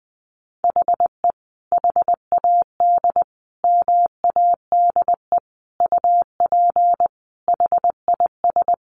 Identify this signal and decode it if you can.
Morse code, heard as a keyed tone, about 20 wpm, 700 Hz. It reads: HE HAD MADE UP HIS
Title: Sentences from Top 500 Words 20wpm